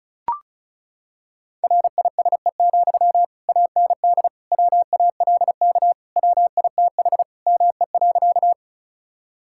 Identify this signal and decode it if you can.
Morse code, heard as a keyed tone, about 36 wpm, 700 Hz. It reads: RISE, AND WALK WITH ME.
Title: Sentences from Top 1000 Words - Review Part 6 - 35wpm